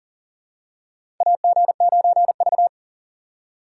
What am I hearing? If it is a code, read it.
Morse code, heard as a keyed tone, about 40 wpm, 700 Hz. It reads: AG9V
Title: USA Call Signs - 4x Character Spacing - 40wpm